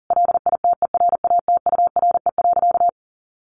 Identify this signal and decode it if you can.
Morse code, heard as a keyed tone, about 40 wpm, 700 Hz. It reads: LITERATURE.
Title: Sentences from Top 900 Words 40wpm